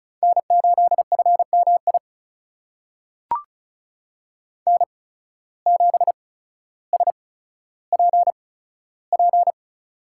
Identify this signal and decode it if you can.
Morse code, heard as a keyed tone, about 36 wpm, spaced as for 15 wpm, 700 Hz. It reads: N8FMI N7SPP
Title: USA Call Signs - 5 Characters - 8x Character Spacing - 35wpm